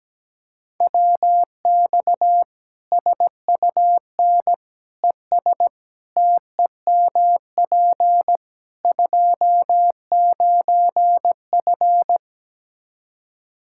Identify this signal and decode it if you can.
Morse code, heard as a keyed tone, about 17 wpm, 700 Hz. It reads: WX SUN ES TEMP 29F
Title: K1USN SST QSO Practice 17wpm